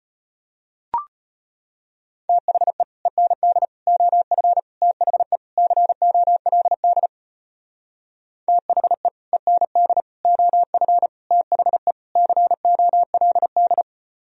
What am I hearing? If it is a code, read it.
Morse code, heard as a keyed tone, about 32 wpm, 700 Hz. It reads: THE END OF THE COLD THE END OF THE COLD